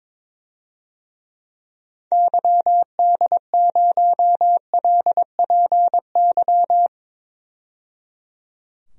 Morse code, heard as a keyed tone, about 21 wpm, 700 Hz. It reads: YD0LPY
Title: DXCC-100 Call Signs - 6 Characters - 22wpm